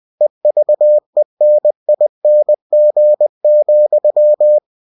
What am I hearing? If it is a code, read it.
Morse code, heard as a keyed tone, about 20 wpm, 600 Hz. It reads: EVENING,
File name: cw-books / winnie-the-pooh / winnie-20wpm-page-0110